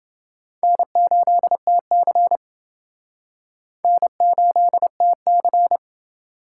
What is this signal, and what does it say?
Morse code, heard as a keyed tone, about 28 wpm, 700 Hz. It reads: N8TC N8TC